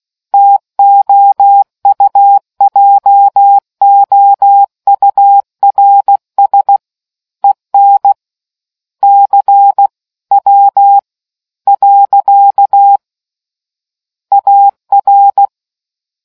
Morse code, heard as a keyed tone, about 16 wpm, 800 Hz. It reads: TOUJOURS EN C W . AR